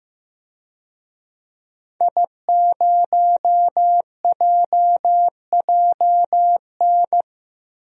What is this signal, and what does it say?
Morse code, heard as a keyed tone, about 15 wpm, 700 Hz. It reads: I0JJN
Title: DXCC-100 Call Signs - 5 Characters - 15wpm